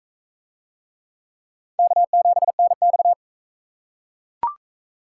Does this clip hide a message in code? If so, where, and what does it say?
Yes: Morse code from 1.8 to 3.1 s, heard as a keyed tone, about 39 wpm, 700 Hz. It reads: K7NX